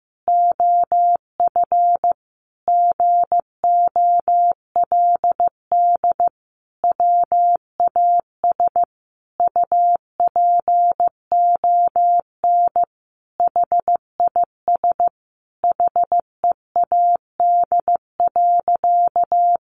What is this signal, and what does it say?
Morse code, heard as a keyed tone, about 15 wpm, 700 Hz. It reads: OF GOLD WAS UPON HIS HEAD.
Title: Sentences from Top 1100 Words 15wpm